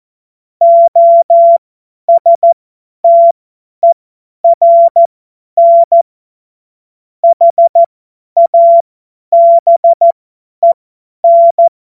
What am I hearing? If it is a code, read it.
Morse code, heard as a keyed tone, about 14 wpm, spaced as for 10 wpm, 680 Hz. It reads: OSTERN HABEN
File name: Morsekurs / Witze / Witz 20. Juni 2023 50 F70